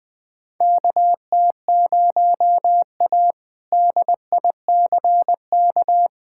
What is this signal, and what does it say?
Morse code, heard as a keyed tone, about 20 wpm, 700 Hz. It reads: KT0A DICK
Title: K1USN SST QSO Practice - 20wpm